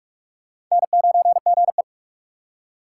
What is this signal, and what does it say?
Morse code, heard as a keyed tone, about 42 wpm, 700 Hz. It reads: N9GE
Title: USA Call Signs - 4 Characters - 4x Character Spacing - 45wpm